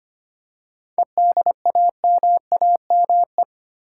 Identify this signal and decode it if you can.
Morse code, heard as a keyed tone, about 25 wpm, 700 Hz. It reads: EDAMAME
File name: practice / 25wpm / Difficult Words with Hints - 25wpm